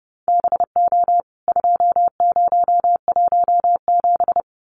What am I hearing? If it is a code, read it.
Morse code, heard as a keyed tone, about 30 wpm, 700 Hz. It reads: BO 2017
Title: CWT Exchange 30wpm